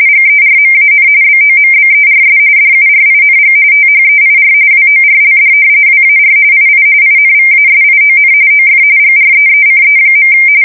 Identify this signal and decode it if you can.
RTTY, about 45.45 baud. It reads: GERMAN BOMBERS DURING THEIR WORLD WAR II INVASION OF GREECE.
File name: fsk170 / battleships